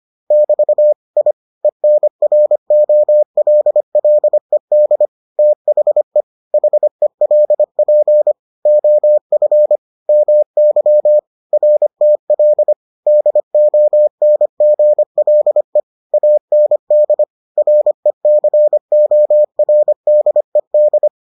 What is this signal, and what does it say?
Morse code, heard as a keyed tone, about 25 wpm, 600 Hz. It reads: <BT> I ENROLLED THE HELP OF MY RTL DONGLE AND RECORDED